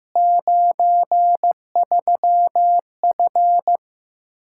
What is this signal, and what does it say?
Morse code, heard as a keyed tone, about 15 wpm, 700 Hz. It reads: 93F
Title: K1USN SST QSO Practice 15wpm